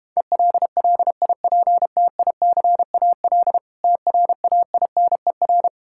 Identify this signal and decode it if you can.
Morse code, heard as a keyed tone, about 32 wpm, 700 Hz. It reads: ELLIPTICAL TRAINER